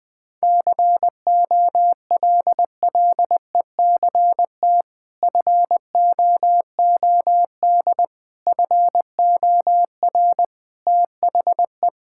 Morse code, heard as a keyed tone, about 20 wpm, 700 Hz. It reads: COLLECT FOOD FOR THE
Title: Sentences from Top 1600 Words 20wpm